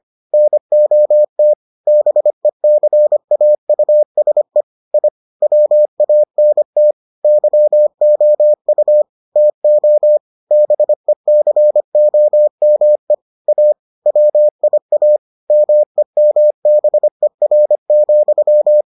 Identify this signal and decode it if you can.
Morse code, heard as a keyed tone, about 25 wpm, 600 Hz. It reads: NOT BECAUSE I WANT YOU TO BECOME A WIA MEMBER,